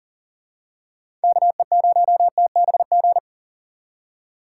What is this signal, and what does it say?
Morse code, heard as a keyed tone, about 40 wpm, 700 Hz. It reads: KE0TBG
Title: USA Call Signs - 6 Characters - 6x Character Spacing - 40wpm